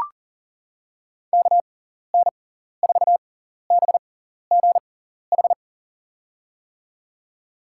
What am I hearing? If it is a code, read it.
Morse code, heard as a keyed tone, about 40 wpm, spaced as for 14 wpm, 700 Hz. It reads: KN4BGH
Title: USA Call Signs - 6 Characters - 6x Character Spacing - 40wpm